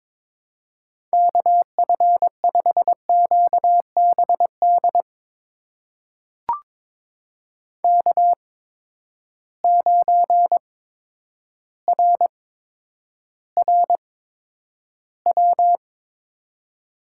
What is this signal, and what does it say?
Morse code, heard as a keyed tone, about 22 wpm, spaced as for 10 wpm, 700 Hz. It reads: KF5QBD K9RRW
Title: USA Call Signs - 8x Character Spacing - 22wpm